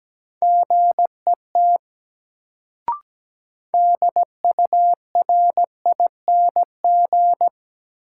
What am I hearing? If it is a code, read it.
Morse code, heard as a keyed tone, about 17 wpm, 700 Hz. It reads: GET DURING
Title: Sets of 2 Words - New Word Drill - 17wpm